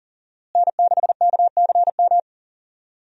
Morse code, heard as a keyed tone, about 37 wpm, 700 Hz. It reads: N6KCM
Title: USA Call Signs - 5 Characters - Repeated 3x - 40wpm